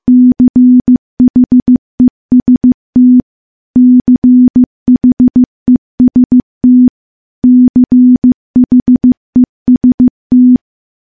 Morse code, heard as a keyed tone, about 15 wpm, 250 Hz. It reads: CHEST CHEST CHEST
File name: morse-training-15WPM